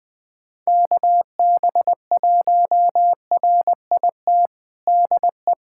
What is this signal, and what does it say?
Morse code, heard as a keyed tone, about 20 wpm, 700 Hz. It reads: KB1RIT DE
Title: USA Callsign DE Callsign - 6 Characters - 20wpm